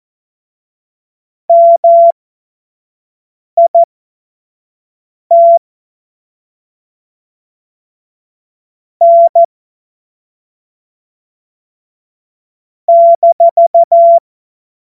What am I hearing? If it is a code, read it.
Morse code, heard as a keyed tone, about 14 wpm, spaced as for 5 wpm, 680 Hz. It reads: MIT N -
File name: Morsekurs / Witze / Witz 07. Dezember 2021, 25 F70